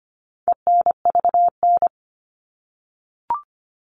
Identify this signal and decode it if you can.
Morse code, heard as a keyed tone, about 25 wpm, 700 Hz. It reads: ENVN
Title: All Cut Numbers - 4-Digit 25wpm